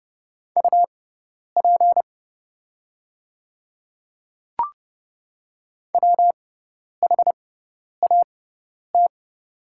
Morse code, heard as a keyed tone, about 30 wpm, spaced as for 10 wpm, 700 Hz. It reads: UP WHAT
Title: 6x Character Spacing - Top 100 Words - 30wpm